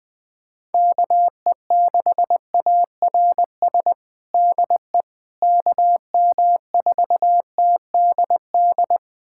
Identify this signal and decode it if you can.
Morse code, heard as a keyed tone, about 19 wpm, 700 Hz. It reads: KE6ARS DE KM4TDD
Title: USA Callsign DE Callsign - 6 Characters - 20wpm